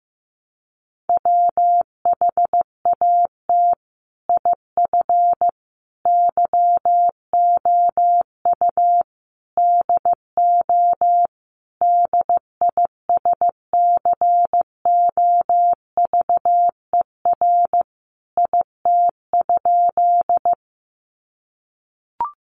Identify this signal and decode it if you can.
Morse code, heard as a keyed tone, about 15 wpm, 700 Hz. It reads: WHAT IF YOU DO DISCOVER IT?
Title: Sentences from Top 1100 Words 15wpm